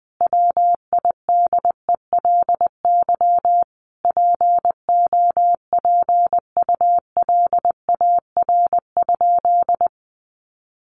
Morse code, heard as a keyed tone, about 20 wpm, 700 Hz. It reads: WIDELY POPULAR?
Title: Sentences from Top 1200 Words 20wpm